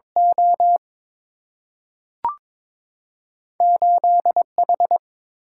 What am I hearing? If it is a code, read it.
Morse code, heard as a keyed tone, about 22 wpm, 700 Hz. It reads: O 8 H
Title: Single-Letter Single-Number 22wpm